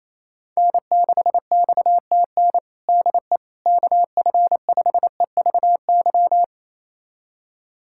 Morse code, heard as a keyed tone, about 27 wpm, 700 Hz. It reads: N6XTN DE KF5EVY
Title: USA Callsign DE Callsign 28wpm